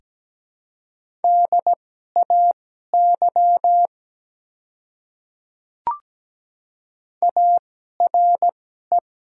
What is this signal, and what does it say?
Morse code, heard as a keyed tone, about 17 wpm, spaced as for 12 wpm, 700 Hz. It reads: DAY ARE
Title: 2x Character Spacing - Top 300 Words - 17wpm